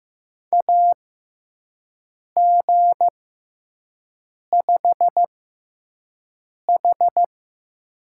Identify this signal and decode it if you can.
Morse code, heard as a keyed tone, about 15 wpm, spaced as for 5 wpm, 700 Hz. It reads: AG5H
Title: USA Call Signs - 4 Characters - 6x Character Spacing - 15wpm